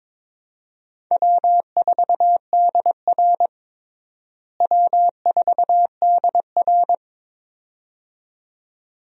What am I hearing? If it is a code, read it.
Morse code, heard as a keyed tone, about 21 wpm, 700 Hz. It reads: W4DR W4DR
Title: USA Call Signs - 4 Characters - Repeated 2x - 22wpm